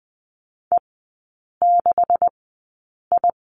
Morse code, heard as a keyed tone, about 20 wpm, spaced as for 8 wpm, 700 Hz. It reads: E6I
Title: 133 Lesson 22e - Question Mark - Three Characters 20wpm